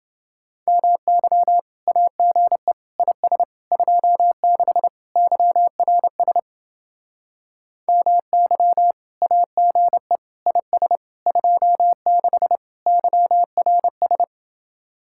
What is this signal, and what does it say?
Morse code, heard as a keyed tone, about 30 wpm, 700 Hz. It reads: MY AGE IS 26 YRS MY AGE IS 26 YRS